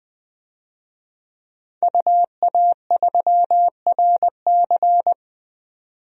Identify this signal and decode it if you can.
Morse code, heard as a keyed tone, about 20 wpm, 700 Hz. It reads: UA3RC
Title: DXCC-100 Call Signs - 5 Characters - 20wpm